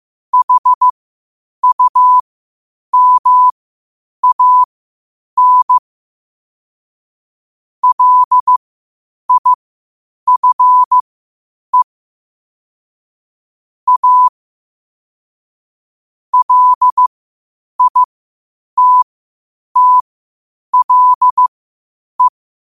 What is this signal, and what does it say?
Morse code, heard as a keyed tone, about 15 wpm, spaced as for 8 wpm, 1000 Hz. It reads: HUMAN LIFE A LITTLE
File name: qotdfiles / 8wpm